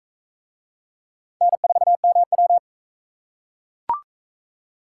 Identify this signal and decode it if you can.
Morse code, heard as a keyed tone, about 39 wpm, 700 Hz. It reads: N4MW